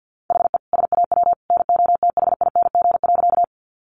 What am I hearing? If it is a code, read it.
Morse code, heard as a keyed tone, about 50 wpm, 700 Hz. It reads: HE SAW NOTHING.
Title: Sentences from Top 500 Words 50wpm